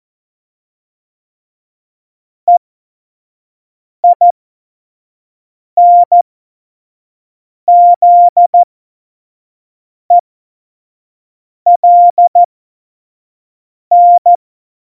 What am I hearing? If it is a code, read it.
Morse code, heard as a keyed tone, about 14 wpm, spaced as for 5 wpm, 700 Hz. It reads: EINZELN
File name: Morsekurs / Witze / Witz 15. Dezember, F70